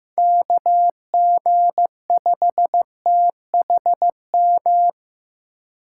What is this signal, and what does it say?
Morse code, heard as a keyed tone, about 18 wpm, 700 Hz. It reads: KG5THM
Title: Speed Racing - USA 2x3 Call Signs 15wpm